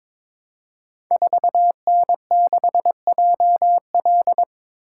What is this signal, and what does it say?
Morse code, heard as a keyed tone, about 22 wpm, 700 Hz. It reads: 4N6JL
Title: Code Groups - 5 Letter:Numbers - 22wpm